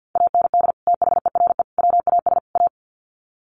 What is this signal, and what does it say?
Morse code, heard as a keyed tone, about 50 wpm, 700 Hz. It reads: AND THERE WAS A
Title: Sets of 4 Words 50wpm